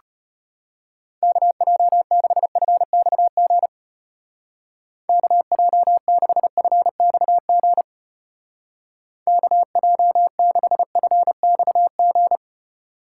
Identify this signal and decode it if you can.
Morse code, heard as a keyed tone, about 31 wpm, 700 Hz. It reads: KJ6FXG KJ6FXG KJ6FXG
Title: Speed Racing - USA 2x3 Call Signs 25wpm